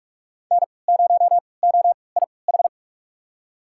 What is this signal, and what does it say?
Morse code, heard as a keyed tone, about 45 wpm, spaced as for 25 wpm, 700 Hz. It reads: N0OIH
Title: USA Call Signs - 5 Characters - 3x Character Spacing - 45wpm